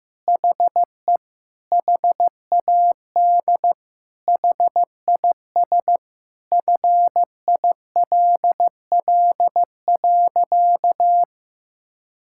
Morse code, heard as a keyed tone, about 15 wpm, 700 Hz. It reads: HE HAD HIS FILL.
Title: Sentences from Top 1000 Words - Review Part 5 - 15wpm